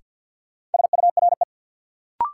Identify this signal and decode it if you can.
Morse code, heard as a keyed tone, about 52 wpm, 700 Hz. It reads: SURE
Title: Sets of 3 Words - New Word Drill - 50wpm